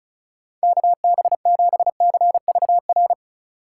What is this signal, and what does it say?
Morse code, heard as a keyed tone, about 34 wpm, 690 Hz. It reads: KB7CVR
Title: USA Call Signs - 6 Characters - 4x Character Spacing - 35wpm